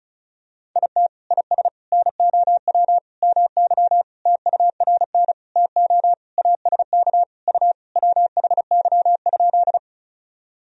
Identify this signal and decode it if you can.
Morse code, heard as a keyed tone, about 34 wpm, 690 Hz. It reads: IT IS NOW MY TURN TO ASK U WHY?
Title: Sentences from Top 200 Words and Abbreviations 35wpm